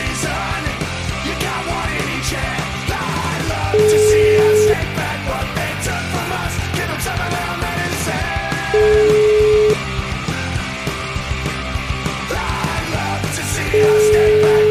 Ringback tone (Europe, 425 Hz), with noise.